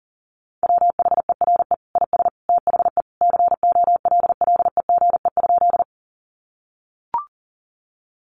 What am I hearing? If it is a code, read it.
Morse code, heard as a keyed tone, about 40 wpm, 700 Hz. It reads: WHERE IS THE COLLEGE?
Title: Sentences from Top 1300 Words 40wpm